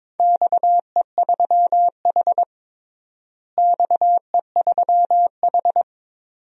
Morse code, heard as a keyed tone, about 22 wpm, 700 Hz. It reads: XE3H XE3H
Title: DXCC-100 Call Signs - 4 Characters - Repeated 2x - 22wpm